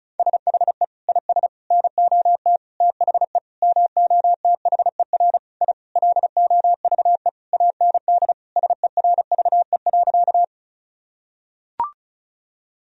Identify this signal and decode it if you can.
Morse code, heard as a keyed tone, about 36 wpm, 700 Hz. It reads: SHE IS NOT THE MOTHER I LOVE AND SERVE.
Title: Sentences from Top 1000 Words - Review Part 2 - 35wpm